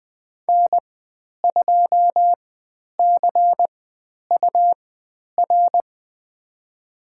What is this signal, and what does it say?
Morse code, heard as a keyed tone, about 20 wpm, spaced as for 10 wpm, 700 Hz. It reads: N2CUR
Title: 109 Lesson 18g - Letter m - Call Signs 20wpm